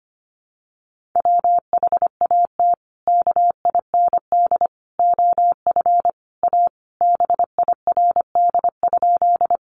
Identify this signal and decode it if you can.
Morse code, heard as a keyed tone, about 25 wpm, 700 Hz. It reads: WHAT KIND OF A BIRD?
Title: Sentences from Top 400 Words 25wpm